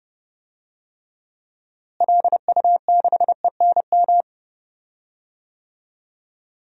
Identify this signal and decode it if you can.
Morse code, heard as a keyed tone, about 32 wpm, 700 Hz. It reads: LU6ENM
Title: DXCC-100 Call Signs - 6 Characters - Repeated 3x - 30wpm